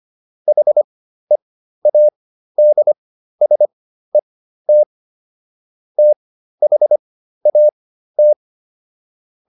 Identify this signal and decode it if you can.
Morse code, heard as a keyed tone, about 25 wpm, spaced as for 13 wpm, 600 Hz. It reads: HEADSET THAT